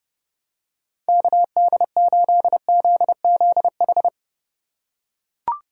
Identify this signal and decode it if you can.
Morse code, heard as a keyed tone, about 31 wpm, 700 Hz. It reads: KD8ZZH
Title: USA Call Signs - 6 Characters - 4x Character Spacing - 30wpm